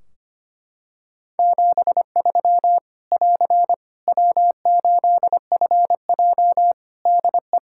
Morse code, heard as a keyed tone, about 25 wpm, 700 Hz. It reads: 73 <AR> W8FJ DE
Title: Practice QSO - Exchange 4 - Ending - 25wpm